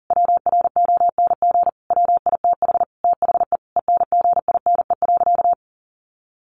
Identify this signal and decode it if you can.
Morse code, heard as a keyed tone, about 37 wpm, 700 Hz. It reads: WRONG WITH THE ENGINE.
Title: Sentences from Top 1500 Words 40wpm